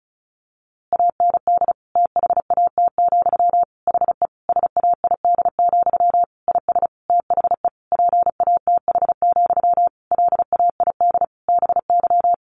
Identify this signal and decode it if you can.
Morse code, heard as a keyed tone, about 34 wpm, 700 Hz. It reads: AND THAT, HE SAID, IS THE PATH, LAID BY
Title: A Sound Of Thunder 35wpm